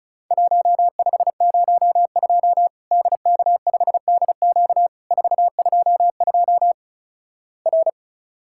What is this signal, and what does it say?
Morse code, heard as a keyed tone, about 36 wpm, 700 Hz. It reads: 1502 DK5DQ 422 R
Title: Worked All Europe DX Contest - QTC Exchange - 35wpm